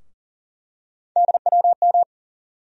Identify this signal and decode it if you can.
Morse code, heard as a keyed tone, about 40 wpm, 700 Hz. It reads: DWM
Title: USA Call Sign Postfixes - 3 Letters 40wpm